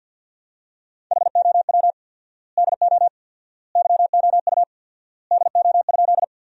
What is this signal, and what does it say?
Morse code, heard as a keyed tone, about 50 wpm, 700 Hz. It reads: HOW DO YOU DO?
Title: 4x Word-Spacing - Sentences from Top 100 Words - 50wpm